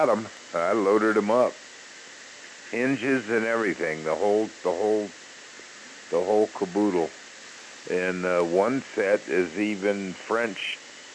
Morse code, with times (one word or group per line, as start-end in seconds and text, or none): none